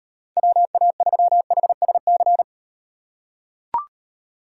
Morse code, heard as a keyed tone, about 38 wpm, 700 Hz. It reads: WA3HSC